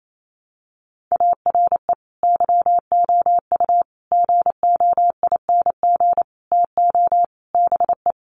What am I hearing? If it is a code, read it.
Morse code, heard as a keyed tone, about 28 wpm, 700 Hz. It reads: ARE YOU GOING TO BE
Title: Sets of 5 Words Encore 28wpm